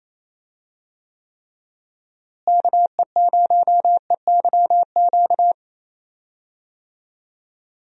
Morse code, heard as a keyed tone, about 28 wpm, 700 Hz. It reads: KE0EYQ
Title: USA Call Signs - 6 Characters - Repeated 2x - 28wpm